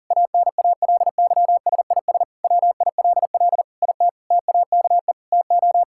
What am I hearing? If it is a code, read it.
Morse code, heard as a keyed tone, about 40 wpm, 700 Hz. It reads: ANALYSIS WILL IT TAKE TO
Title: Sentences from Top 1000 Words - Review Part 5 - 40wpm